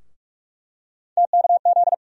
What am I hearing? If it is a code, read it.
Morse code, heard as a keyed tone, about 47 wpm, 700 Hz. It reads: TKZ
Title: USA Call Sign Postfixes - 3 Letters 45wpm